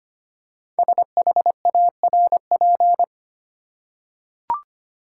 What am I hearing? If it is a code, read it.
Morse code, heard as a keyed tone, about 25 wpm, 700 Hz. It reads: SHARP
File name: 5 Letter Words - First 3 Letters Given - 25wpm